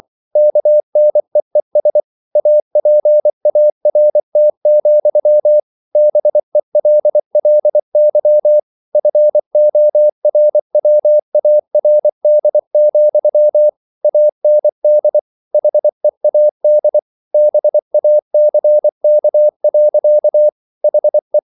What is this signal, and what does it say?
Morse code, heard as a keyed tone, about 24 wpm, 600 Hz. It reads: KNEES APART, BELLY FORWARD, AND HEAD BACK. HE